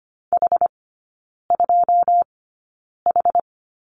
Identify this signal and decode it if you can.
Morse code, heard as a keyed tone, about 25 wpm, spaced as for 9 wpm, 700 Hz. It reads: H2H
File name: Lesson 17e - Letter u - Three Characters 25wpm